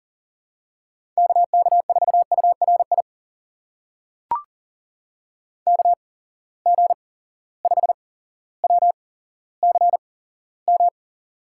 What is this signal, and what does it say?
Morse code, heard as a keyed tone, about 40 wpm, spaced as for 17 wpm, 700 Hz. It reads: KK4URI KG5WCM
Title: USA Call Signs - 6 Characters - 8x Character Spacing - 40wpm